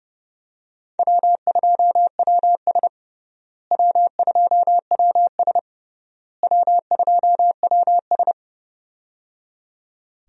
Morse code, heard as a keyed tone, about 30 wpm, 700 Hz. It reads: W2WS W2WS W2WS
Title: USA Call Signs - 4 Characters - Repeated 3x - 30wpm